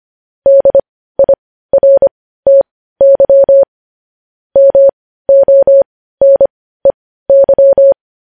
Morse code, heard as a keyed tone, about 25 wpm, spaced as for 15 wpm, 550 Hz. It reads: DIRTY MONEY
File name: podcast / News headlines-1x-25x15